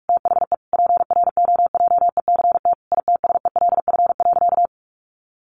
Morse code, heard as a keyed tone, about 44 wpm, 700 Hz. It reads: THE PROJECT ITSELF.
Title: Sentences from Top 1400 Words 45wpm